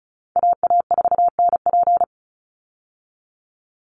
Morse code, heard as a keyed tone, about 34 wpm, 700 Hz. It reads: AA4NP
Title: Call Signs Repeated 2x - 35wpm